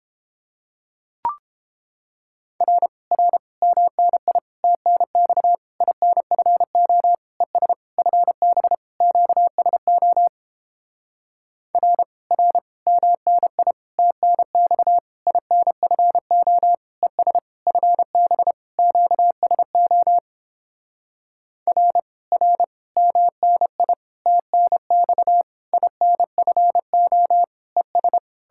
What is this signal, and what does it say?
Morse code, heard as a keyed tone, about 29 wpm, 700 Hz. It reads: R R MNI TNX INFO ES FB QSO R R MNI TNX INFO ES FB QSO R R MNI TNX INFO ES